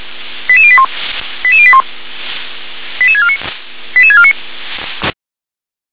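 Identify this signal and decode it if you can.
Five-tone selective call (ZVEI), heard as quick runs of five tones, tones of about 70 ms. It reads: 88081 88081 88649 88649
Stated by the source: ZVEI five-tone selective call 88150 received by SDR.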